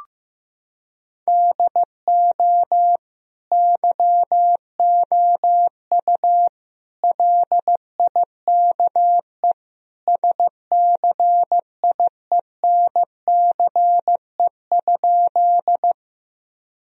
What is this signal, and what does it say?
Morse code, heard as a keyed tone, about 15 wpm, 700 Hz. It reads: DO YOU LIKE SCIENCE?
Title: Sentences from Top 1000 Words - Review Part 2 - 15wpm